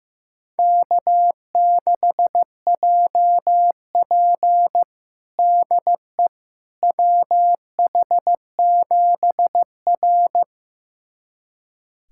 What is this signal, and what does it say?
Morse code, heard as a keyed tone, about 15 wpm, 700 Hz. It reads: K6JP DE WH7R
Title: USA Callsign DE Callsign - 4 Characters - 15wpm